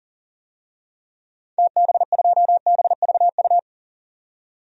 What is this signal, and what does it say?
Morse code, heard as a keyed tone, about 40 wpm, 700 Hz. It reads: TB2BVU
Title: DXCC-100 Call Signs - 6 Characters - 40wpm